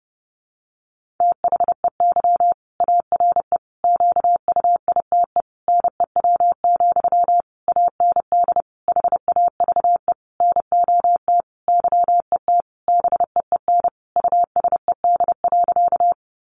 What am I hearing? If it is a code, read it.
Morse code, heard as a keyed tone, about 30 wpm, 700 Hz. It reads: THEY ARE QUITE NEW, AND HAVE NOT YET BEEN USED.
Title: Sentences from Top 700 Words 30wpm